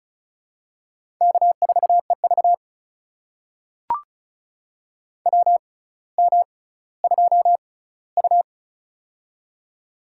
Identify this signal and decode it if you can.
Morse code, heard as a keyed tone, about 36 wpm, spaced as for 18 wpm, 700 Hz. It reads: K4EV WM2U
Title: USA Call Signs - 4 Characters - 6x Character Spacing - 35wpm